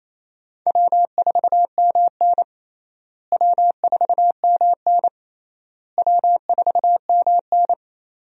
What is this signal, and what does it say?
Morse code, heard as a keyed tone, about 28 wpm, 700 Hz. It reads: W4MN W4MN W4MN
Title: USA Call Signs - 4 Characters - Repeated 3x - 28wpm